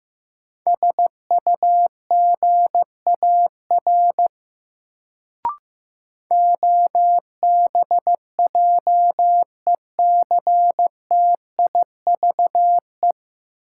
Morse code, heard as a keyed tone, about 15 wpm, 700 Hz. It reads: SUGAR OBJECTIVE
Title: Sentences from Top 1200 Words - New Word Drill - 15wpm